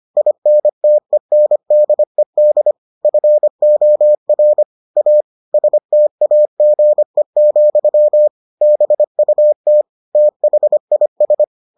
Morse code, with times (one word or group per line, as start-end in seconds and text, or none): 0.2-2.7 INTENDED
3.0-4.6 FOR
5.0-5.2 A
5.5-8.3 STAGE,
8.6-9.8 BUT
10.2-11.4 THIS